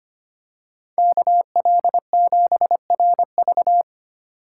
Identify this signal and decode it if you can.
Morse code, heard as a keyed tone, about 25 wpm, 700 Hz. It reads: KL7RV
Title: USA Call Signs - 5 Characters - 3x Character Spacing - 25wpm